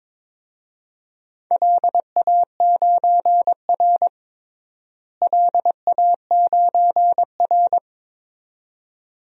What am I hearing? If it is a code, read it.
Morse code, heard as a keyed tone, about 22 wpm, 700 Hz. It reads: LA9R LA9R
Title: DXCC-100 Call Signs - 4 Characters - Repeated 2x - 22wpm